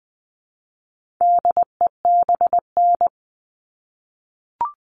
Morse code, heard as a keyed tone, about 20 wpm, 700 Hz. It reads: DEBN
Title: All Cut Numbers - 4-Digit 20wpm